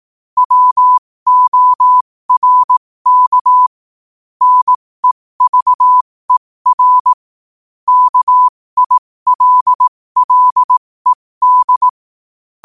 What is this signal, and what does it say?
Morse code, heard as a keyed tone, about 18 wpm, spaced as for 15 wpm, 1000 Hz. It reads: WORK NEVER KILLED
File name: qotdfiles / 15wpm